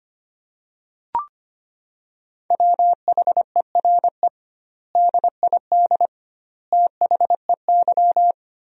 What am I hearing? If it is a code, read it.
Morse code, heard as a keyed tone, about 25 wpm, 700 Hz. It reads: WHERE DID THEY
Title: 2x Word-Spacing - Sentences from Top 400 Words - 25wpm